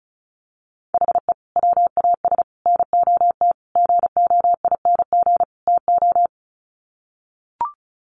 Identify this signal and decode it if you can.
Morse code, heard as a keyed tone, about 34 wpm, 700 Hz. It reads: HE WAS NOT GOING TO
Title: Sets of 5 Words 35wpm